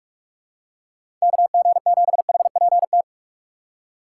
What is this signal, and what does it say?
Morse code, heard as a keyed tone, about 43 wpm, 700 Hz. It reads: KG7HPT
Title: USA Call Signs - 6 Characters - 2x Character Spacing - 45wpm